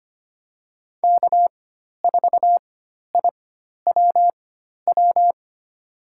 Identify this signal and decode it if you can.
Morse code, heard as a keyed tone, about 25 wpm, spaced as for 12 wpm, 700 Hz. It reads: K4IWW
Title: USA Call Signs - 5 Characters - 4x Character Spacing - 25wpm